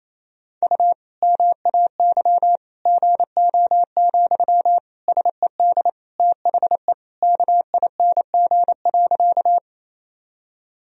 Morse code, heard as a keyed tone, about 28 wpm, 700 Hz. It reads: U MAY GO, SED THE KING.
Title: Sentences from Top 400 Words and Abbreviations 28wpm